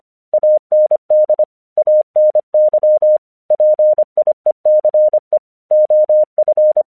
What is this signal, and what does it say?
Morse code, heard as a keyed tone, about 25 wpm, 610 Hz. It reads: AND ANY PIECE OF
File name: 1 English 25wpm / AQC11